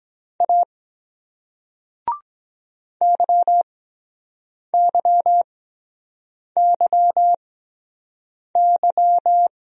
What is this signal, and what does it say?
Morse code, heard as a keyed tone, about 23 wpm, spaced as for 7 wpm, 700 Hz. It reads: A YYYY